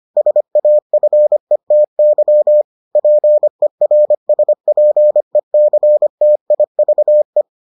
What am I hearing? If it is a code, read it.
Morse code, heard as a keyed tone, about 25 wpm, 600 Hz. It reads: SAFETY PERSPECTIVE